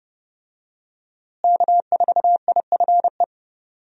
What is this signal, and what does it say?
Morse code, heard as a keyed tone, about 30 wpm, 700 Hz. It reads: K4IFE
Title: USA Call Signs - 5 Characters - Repeated 2x - 30wpm